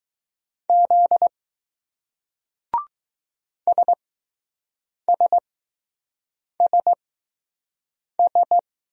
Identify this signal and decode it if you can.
Morse code, heard as a keyed tone, about 20 wpm, spaced as for 6 wpm, 700 Hz. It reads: Z SSSS